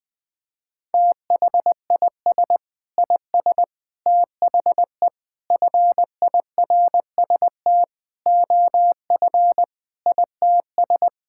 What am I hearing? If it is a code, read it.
Morse code, heard as a keyed tone, about 20 wpm, 700 Hz. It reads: THIS IS THE FIRST OF ITS
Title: Sentences with 6-Word Phrases - 20wpm